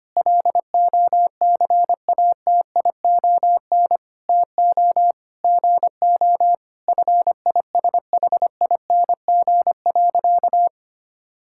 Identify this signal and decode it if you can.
Morse code, heard as a keyed tone, about 25 wpm, 700 Hz. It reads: LOCATION TO GO FISHING.
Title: Sentences from Top 1000 Words - Review Part 6 - 25wpm